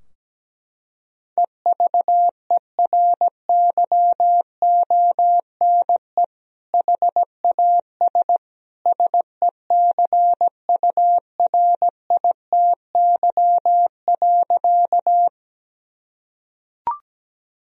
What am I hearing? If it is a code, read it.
Morse code, heard as a keyed tone, about 17 wpm, 700 Hz. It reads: EVERYONE HAS SECURITY.
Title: Sentences from Top 1000 Words - Review Part 4 - 17wpm